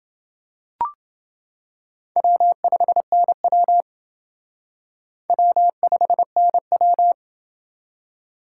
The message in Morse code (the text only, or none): W5NW W5NW